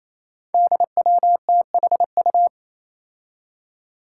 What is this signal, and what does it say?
Morse code, heard as a keyed tone, about 28 wpm, 700 Hz. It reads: DWTHU
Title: Five Challenging Letter-Number Combinations 28wpm